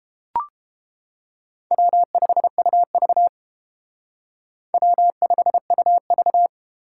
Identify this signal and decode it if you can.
Morse code, heard as a keyed tone, about 31 wpm, 700 Hz. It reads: W5UV W5UV